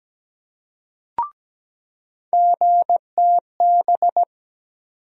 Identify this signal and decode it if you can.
Morse code, heard as a keyed tone, about 17 wpm, 700 Hz. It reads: GTB
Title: USA Call Sign Postfixes - 3 Letters 17wpm